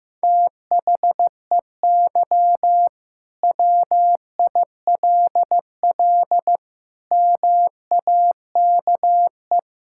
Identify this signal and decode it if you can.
Morse code, heard as a keyed tone, about 16 wpm, 700 Hz. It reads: THEY WILL MAKE
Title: Sentences from Top 1000 Words - Review Part 5 - 15wpm